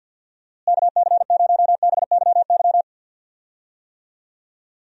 Morse code, heard as a keyed tone, about 49 wpm, 700 Hz. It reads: KC0BYY
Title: USA Call Signs - 6 Characters - Repeated 2x - 50wpm